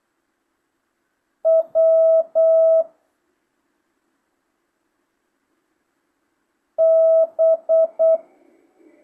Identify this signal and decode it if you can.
Morse code, heard as a keyed tone, about 8 wpm, 650 Hz. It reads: W B
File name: jh8cbh / 2025cw